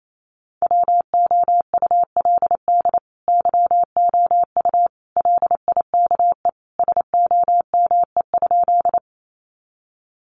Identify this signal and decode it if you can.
Morse code, heard as a keyed tone, about 28 wpm, 700 Hz. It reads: WOULD YOU LIKE SOME?
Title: Sentences from Top 100 Words 28wpm